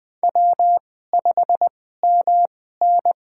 Morse code, heard as a keyed tone, about 20 wpm, spaced as for 14 wpm, 700 Hz. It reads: W5MN
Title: USA Call Signs - 4 Characters - 2x Character Spacing - 20wpm